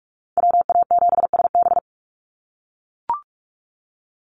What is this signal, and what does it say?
Morse code, heard as a keyed tone, about 44 wpm, 700 Hz. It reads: WA7SB
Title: NA SOTA Chaser Callsigns 45wpm